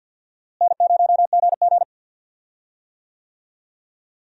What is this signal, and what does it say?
Morse code, heard as a keyed tone, about 52 wpm, 700 Hz. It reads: N0GG
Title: USA Call Signs - 4 Characters - Repeated 3x - 50wpm